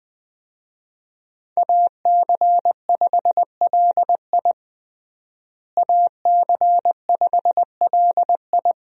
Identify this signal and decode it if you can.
Morse code, heard as a keyed tone, about 20 wpm, 700 Hz. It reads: AC5LI AC5LI
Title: USA Call Signs - 5 Characters - Repeated 2x - 20wpm